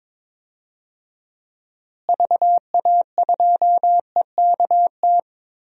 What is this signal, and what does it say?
Morse code, heard as a keyed tone, about 22 wpm, 700 Hz. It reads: VA2EKT
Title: DXCC-100 Call Signs - 6 Characters - Repeated 2x - 22wpm